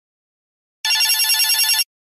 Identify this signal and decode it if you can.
There is no code.